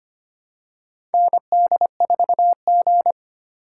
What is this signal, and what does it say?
Morse code, heard as a keyed tone, about 25 wpm, 700 Hz. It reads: ND4G
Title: K1USN SST QSO Practice - 25wpm